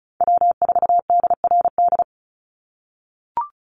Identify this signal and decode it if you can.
Morse code, heard as a keyed tone, about 34 wpm, 700 Hz. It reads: W4DRD
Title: USA Call Signs 35wpm